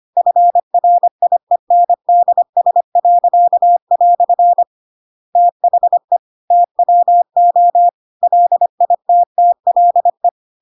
Morse code, heard as a keyed tone, about 25 wpm, 700 Hz. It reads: FRIENDS." THE TWO LITTLE
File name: Pinocchio / Pinocchio ch2 25 25 ik1hge